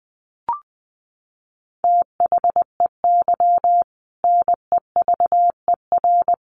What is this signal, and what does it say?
Morse code, heard as a keyed tone, about 20 wpm, 700 Hz. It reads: THEY NEVER
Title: Sentences from Top 1500 Words 20wpm